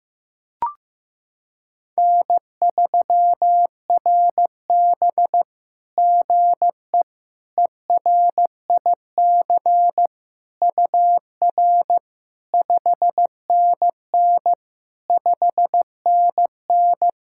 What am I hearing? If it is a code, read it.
Morse code, heard as a keyed tone, about 15 wpm, 700 Hz. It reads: N3RB GE ERIC UR 5NN 5NN
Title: SOTA QSO Practice 15wpm